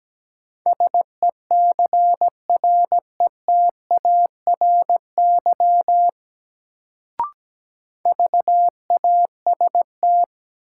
Morse code, heard as a keyed tone, about 17 wpm, 700 Hz. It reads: SECRETARY VAST
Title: Sentences from Top 1300 Words - New Word Drill Alt - 17wpm